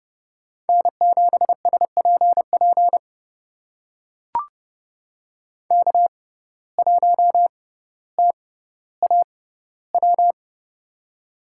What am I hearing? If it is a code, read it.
Morse code, heard as a keyed tone, about 30 wpm, spaced as for 15 wpm, 700 Hz. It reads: N7SPP K1TAW
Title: USA Call Signs - 5 Characters - 6x Character Spacing - 30wpm